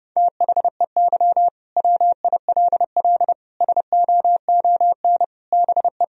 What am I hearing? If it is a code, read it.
Morse code, heard as a keyed tone, about 32 wpm, 700 Hz. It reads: THEY WILL SOON BE
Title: Sentences from Top 1000 Words - Review Part 4 - 30wpm